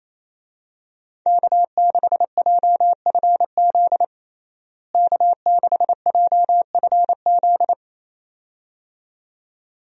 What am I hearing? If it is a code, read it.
Morse code, heard as a keyed tone, about 28 wpm, 700 Hz. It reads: K6JFZ K6JFZ
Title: USA Call Signs - 5 Characters - Repeated 2x - 28wpm